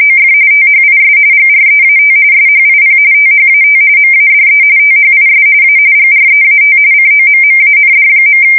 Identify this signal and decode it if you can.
RTTY, about 45.45 baud. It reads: OF THIS GREAT SHIP THAT MADE THIS A GREAT COMMAND.
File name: fsk170 / USS-Missouri